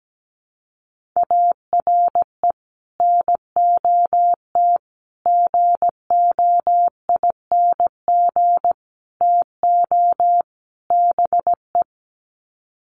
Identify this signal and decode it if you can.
Morse code, heard as a keyed tone, about 17 wpm, 700 Hz. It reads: ARE NOT GOING TO BE
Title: Sets of 5 Words Encore 17wpm